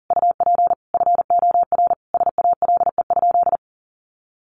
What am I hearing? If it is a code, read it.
Morse code, heard as a keyed tone, about 37 wpm, 700 Hz. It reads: UP FOR SALE?
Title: Sentences from Top 1400 Words 40wpm